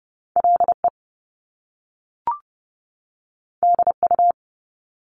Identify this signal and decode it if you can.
Morse code, heard as a keyed tone, about 30 wpm, 700 Hz. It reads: LE DU
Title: Top Two Letter Combinations in Words 30wpm